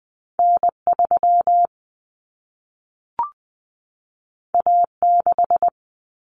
Morse code, heard as a keyed tone, about 20 wpm, 700 Hz. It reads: N3 A6
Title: Common Cut Numbers - 2-Digit 20wpm